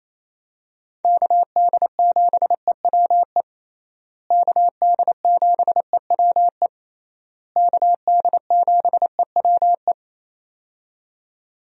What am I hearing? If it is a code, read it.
Morse code, heard as a keyed tone, about 28 wpm, 700 Hz. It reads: KD7EWE KD7EWE KD7EWE
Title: USA Call Signs - 6 Characters - Repeated 3x - 28wpm